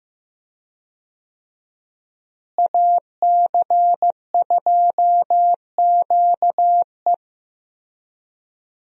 Morse code, heard as a keyed tone, about 15 wpm, 700 Hz. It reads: AC2QE
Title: USA Call Signs - 5 Characters - Repeated 2x - 15wpm